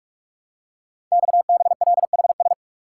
Morse code, heard as a keyed tone, about 44 wpm, 700 Hz. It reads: XBLHS
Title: Code Groups - 5 Letters - 45wpm